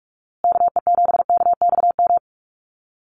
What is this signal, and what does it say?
Morse code, heard as a keyed tone, about 44 wpm, 690 Hz. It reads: KE7KXM
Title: USA Call Signs Repeated 2x 45wpm